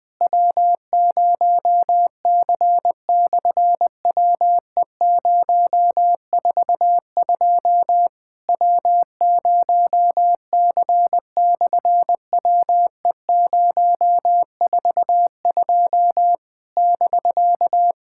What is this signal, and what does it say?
Morse code, heard as a keyed tone, about 19 wpm, 700 Hz. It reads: W0C/WE042 W0C/WE042 <BK>
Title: SOTA QSO Practice 20wpm